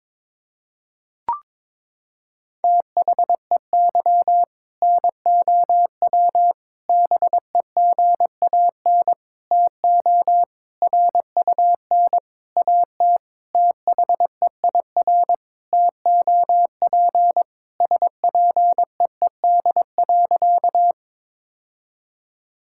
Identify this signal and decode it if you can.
Morse code, heard as a keyed tone, about 22 wpm, 700 Hz. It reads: THEY NOW BEGAN TO RUN AT THEIR TOP SPEED.
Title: Sentences from Top 1000 Words - Review Part 5 - 22wpm